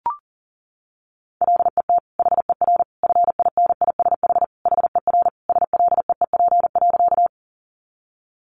Morse code, heard as a keyed tone, about 40 wpm, 700 Hz. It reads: LET HER FINISH HER SLEEP.
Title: Sentences from Top 1200 Words 40wpm